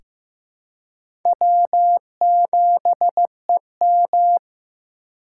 Morse code, heard as a keyed tone, about 16 wpm, 700 Hz. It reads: W7EM
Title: USA Call Signs - 4 Characters - 4x Character Spacing - 15wpm